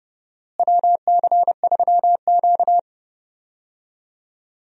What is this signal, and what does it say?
Morse code, heard as a keyed tone, about 30 wpm, 700 Hz. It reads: WC3Q
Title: USA Call Signs - 4 Characters - Repeated 2x - 30wpm